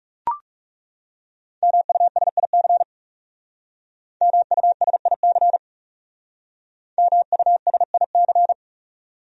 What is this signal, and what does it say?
Morse code, heard as a keyed tone, about 37 wpm, 700 Hz. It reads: MUSIC MUSIC MUSIC
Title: Speed-Racing Top 300 Words 30wpm